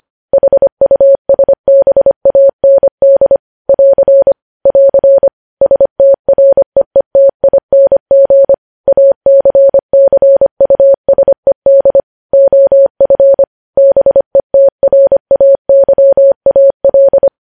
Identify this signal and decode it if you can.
Morse code, heard as a keyed tone, about 25 wpm, 550 Hz. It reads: HUSBAND <AR> <AR> STREETING ACCUSED OF BETRAYAL
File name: podcast / News headlines-1x-25x25